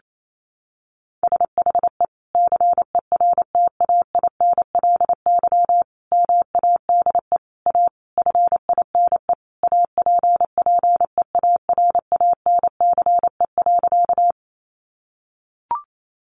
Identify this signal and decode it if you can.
Morse code, heard as a keyed tone, about 28 wpm, 700 Hz. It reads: SHE CERTAINLY MADE A FINE APPEARANCE.
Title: Sentences from Top 1100 Words 28wpm